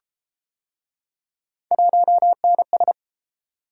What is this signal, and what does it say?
Morse code, heard as a keyed tone, about 31 wpm, 700 Hz. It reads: 1NS